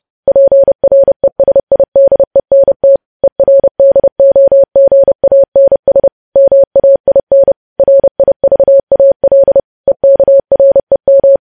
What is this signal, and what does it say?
Morse code, heard as a keyed tone, about 30 wpm, 550 Hz. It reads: PRESIDENT ERDOGANS MAIN RIVAL EKREM
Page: news headlines in morse code at 30 wpm